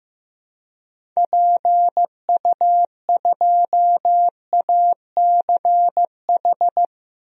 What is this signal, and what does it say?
Morse code, heard as a keyed tone, about 16 wpm, 700 Hz. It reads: PU2ACH
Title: DXCC-100 Call Signs - 6 Characters - Repeated 3x - 15wpm